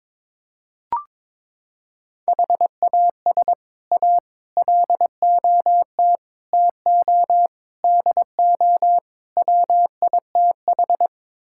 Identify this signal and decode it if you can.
Morse code, heard as a keyed tone, about 22 wpm, 700 Hz. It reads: HAS A LOT TO DO WITH
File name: practice / 22wpm / Sets of 6 Words - 22wpm